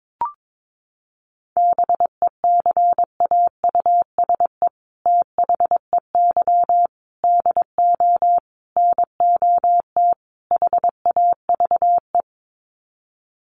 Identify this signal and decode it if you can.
Morse code, heard as a keyed tone, about 22 wpm, 700 Hz. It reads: BECAUSE THEY DO NOT HAVE
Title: Sets of 5 Words Encore2 22wpm